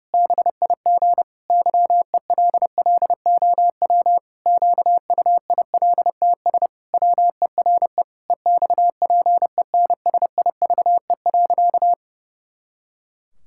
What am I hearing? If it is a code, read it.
Morse code, heard as a keyed tone, about 30 wpm, 700 Hz. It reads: BIG YELLOW QUILTS WERE EXPENSIVE.
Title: Pangram Challenge v2 - 30wpm